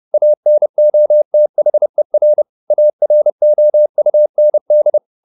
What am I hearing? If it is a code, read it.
Morse code, heard as a keyed tone, about 30 wpm, 600 Hz. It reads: ANOTHER AROUND